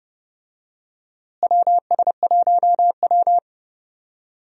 Morse code, heard as a keyed tone, about 30 wpm, 700 Hz. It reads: WS1W
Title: USA Call Signs - 4 Characters - Repeated 2x - 30wpm